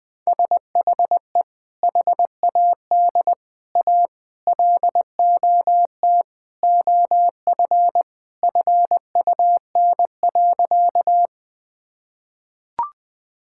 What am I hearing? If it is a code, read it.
Morse code, heard as a keyed tone, about 20 wpm, 700 Hz. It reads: SHE HAD A LOT OF FUN.
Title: Sentences with 4-Word Phrases - Spoken Beginning - 20wpm